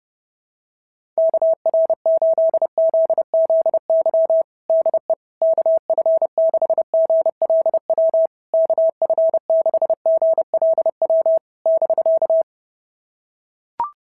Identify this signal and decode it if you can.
Morse code, heard as a keyed tone, about 30 wpm, 650 Hz. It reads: KR8ZZY DE KF6GLW KF6GLW <BK>
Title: SOTA QSO Practice 30wpm